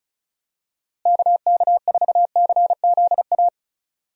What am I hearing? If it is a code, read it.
Morse code, heard as a keyed tone, about 34 wpm, 700 Hz. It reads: KK4CZA
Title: USA Call Signs - 6 Characters - Repeated 3x - 35wpm